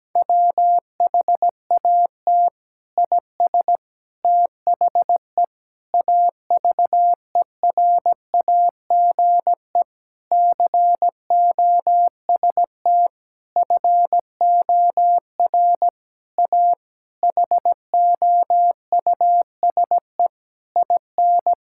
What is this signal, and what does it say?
Morse code, heard as a keyed tone, about 17 wpm, 700 Hz. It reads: WHAT IS THE AVERAGE COST FOR A HOUSE IN
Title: Sentences from Top 1000 Words - Review Part 6 - 17wpm